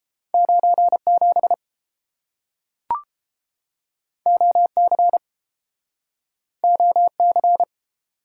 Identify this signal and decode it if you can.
Morse code, heard as a keyed tone, about 30 wpm, 700 Hz. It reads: 97 OC OC